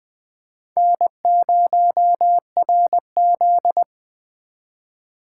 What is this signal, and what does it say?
Morse code, heard as a keyed tone, about 21 wpm, 700 Hz. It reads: N0RZ
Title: 2023 SOTA Chaser Call Signs - 20wpm